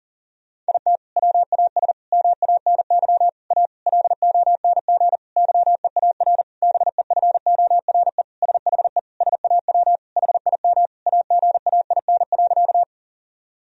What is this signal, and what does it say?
Morse code, heard as a keyed tone, about 40 wpm, 700 Hz. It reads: IT WAS MANY A LONG YEAR BEFORE SHE SAW HIM AGAIN.
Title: Sentences from Top 1000 Words - Review Part 1 - 40wpm